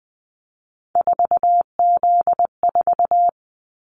Morse code, heard as a keyed tone, about 20 wpm, 700 Hz. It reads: 4Z4
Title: Callsign Prefixes for Top 50 DXCC - 20wpm